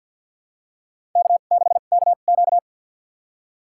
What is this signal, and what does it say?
Morse code, heard as a keyed tone, about 52 wpm, spaced as for 36 wpm, 700 Hz. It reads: K6KQ
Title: USA Call Signs - 4 Characters - 2x Character Spacing - 50wpm